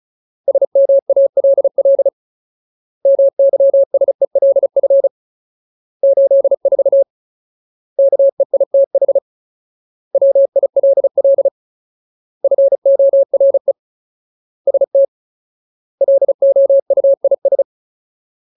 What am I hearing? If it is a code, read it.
Morse code, heard as a keyed tone, about 34 wpm, 550 Hz. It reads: SMALL MYSELF 84 KEITH WILL FORE ST LOUIS
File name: Ultimate Word List / 35WPM Ultimate